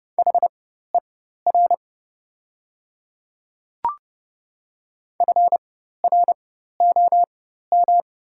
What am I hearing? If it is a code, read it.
Morse code, heard as a keyed tone, about 29 wpm, spaced as for 14 wpm, 690 Hz. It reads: HER FROM
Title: 4x Character Spacing - Top 100 Words - 30wpm